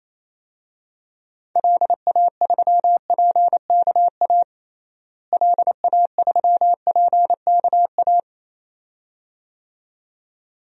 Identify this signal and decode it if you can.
Morse code, heard as a keyed tone, about 28 wpm, 700 Hz. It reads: LA3PKA LA3PKA
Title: DXCC-100 Call Signs - 6 Characters - Repeated 2x - 28wpm